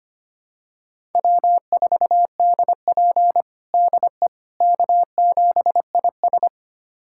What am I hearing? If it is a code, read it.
Morse code, heard as a keyed tone, about 25 wpm, 700 Hz. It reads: W4DP DE K7IS
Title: USA Callsign DE Callsign - 4 Characters - 25wpm